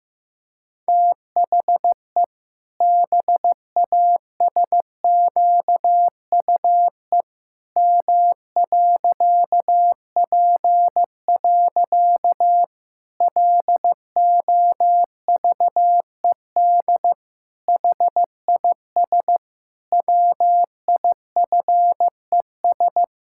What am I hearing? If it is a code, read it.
Morse code, heard as a keyed tone, about 15 wpm, 700 Hz. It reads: THE BASQUE M.P. LOVED HIS WIFES
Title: Pangram Challenge v2 - 15wpm